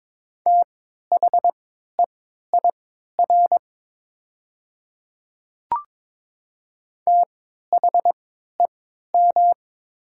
Morse code, heard as a keyed tone, about 22 wpm, spaced as for 12 wpm, 700 Hz. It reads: THEIR THEM
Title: 3x Character Spacing - Top 200 Words - 22wpm